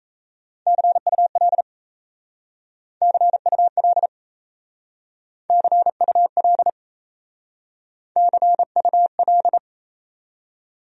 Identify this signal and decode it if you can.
Morse code, heard as a keyed tone, about 35 wpm, 700 Hz. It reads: CUL CUL CUL CUL